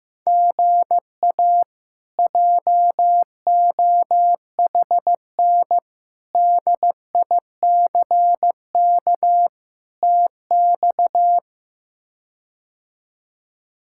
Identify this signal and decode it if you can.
Morse code, heard as a keyed tone, about 15 wpm, 700 Hz. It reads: GA JOHN DICK TX
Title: K1USN SST QSO Practice - 15wpm